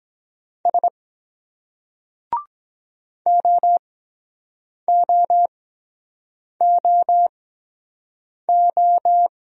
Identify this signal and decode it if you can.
Morse code, heard as a keyed tone, about 23 wpm, spaced as for 7 wpm, 700 Hz. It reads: S OOOO